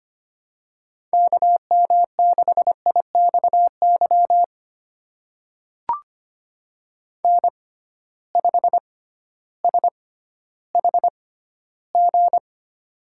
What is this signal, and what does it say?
Morse code, heard as a keyed tone, about 25 wpm, spaced as for 13 wpm, 700 Hz. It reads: KM6IXY N5SHG
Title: USA Call Signs - 6x Character Spacing - 25wpm